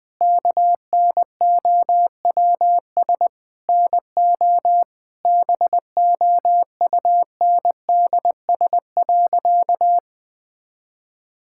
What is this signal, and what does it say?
Morse code, heard as a keyed tone, about 20 wpm, 700 Hz. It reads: KNOWS NO BOUNDS.
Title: Difficult Sentences with Hints - 20wpm